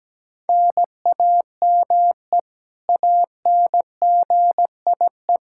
Morse code, heard as a keyed tone, about 17 wpm, 700 Hz. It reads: NAME ANGIE
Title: Practice QSO - Exchange 1 - Beginning - 17wpm